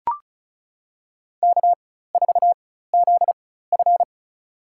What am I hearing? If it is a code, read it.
Morse code, heard as a keyed tone, about 36 wpm, spaced as for 16 wpm, 700 Hz. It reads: K4ZF
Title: USA Call Signs - 4 Characters - 4x Character Spacing - 35wpm